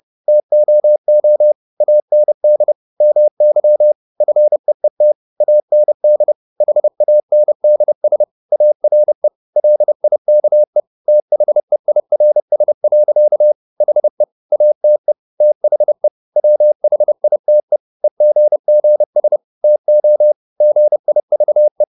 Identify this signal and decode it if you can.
Morse code, heard as a keyed tone, about 30 wpm, 600 Hz. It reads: TOO AND MY FEET AND HANDS ARE LIKE THEIRS. HE ATE THE WHITE EGGS TO GIVE
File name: the old man and the sea / the old man and the sea 1 30wpm